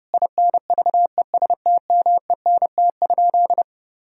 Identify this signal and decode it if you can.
Morse code, heard as a keyed tone, about 30 wpm, 700 Hz. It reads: INVESTMENT?
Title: Sentences from Top 1000 Words - Review Part 4 - 30wpm